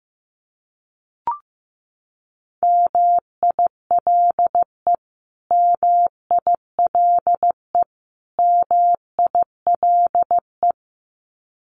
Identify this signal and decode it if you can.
Morse code, heard as a keyed tone, about 15 wpm, 700 Hz. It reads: MILE MILE MILE
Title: Top 300 Words Repeated 15wpm